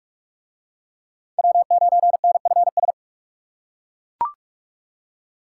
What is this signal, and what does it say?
Morse code, heard as a keyed tone, about 41 wpm, 700 Hz. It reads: W9NFS